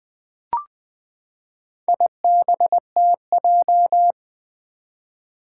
Morse code, heard as a keyed tone, about 20 wpm, 700 Hz. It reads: IBTJ
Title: Four Challenging Letter-Number Combinations 20wpm